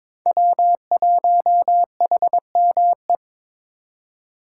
Morse code, heard as a keyed tone, about 26 wpm, 700 Hz. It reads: W1HME
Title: Speed Racing - USA 1x3 Call Signs 22wpm